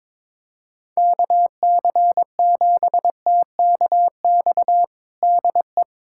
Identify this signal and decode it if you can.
Morse code, heard as a keyed tone, about 22 wpm, 700 Hz. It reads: KC7TKX DE
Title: USA Callsign DE Callsign - 6 Characters - 22wpm